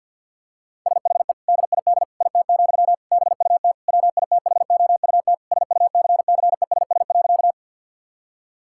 Morse code, heard as a keyed tone, about 52 wpm, 700 Hz. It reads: SHE DID IT, BUT WITHOUT SUCCESS.
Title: Sentences from Top 1000 Words - Review Part 6 - 50wpm